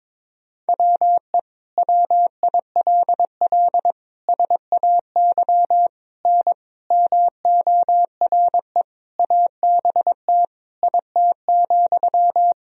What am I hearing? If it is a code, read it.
Morse code, heard as a keyed tone, about 21 wpm, 700 Hz. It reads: WE WILL SAY N MORE ABT IT,
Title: Sentences from Top 200 Words and Abbreviations 22wpm